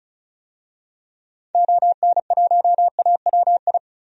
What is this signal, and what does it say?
Morse code, heard as a keyed tone, about 34 wpm, 700 Hz. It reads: ON1AWI
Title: DXCC-100 Call Signs - 6 Characters - 35wpm